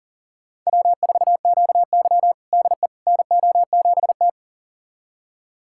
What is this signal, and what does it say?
Morse code, heard as a keyed tone, about 40 wpm, 700 Hz. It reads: W4QY DE NO7T
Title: USA Callsign DE Callsign - 4 Characters - 40wpm